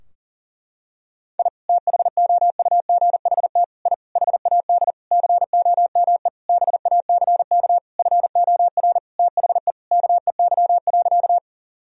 Morse code, heard as a keyed tone, about 40 wpm, 700 Hz. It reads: I THOUGHT I HAD COME BACK FOR THE KEY.
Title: Sentences from Top 1000 Words - Review Part 4 - 40wpm